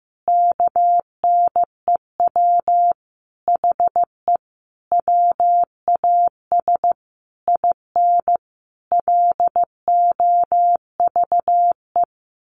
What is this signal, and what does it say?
Morse code, heard as a keyed tone, about 15 wpm, 700 Hz. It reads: KNEW HE WAS IN LOVE
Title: Sentences from Top 1300 Words 15wpm